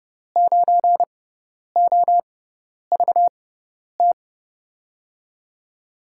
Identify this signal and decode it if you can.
Morse code, heard as a keyed tone, about 30 wpm, spaced as for 10 wpm, 700 Hz. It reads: 9OVT
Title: USA Call Signs - 5 Characters - 6x Character Spacing - 30wpm